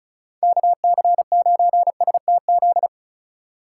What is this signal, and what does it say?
Morse code, heard as a keyed tone, about 45 wpm, 700 Hz. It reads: KC9STZ